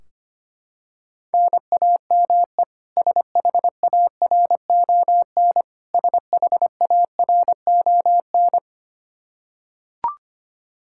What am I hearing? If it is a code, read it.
Morse code, heard as a keyed tone, about 25 wpm, 700 Hz. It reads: NAME SHARON SHARON
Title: Practice QSO - Exchange 1 - Beginning - 25wpm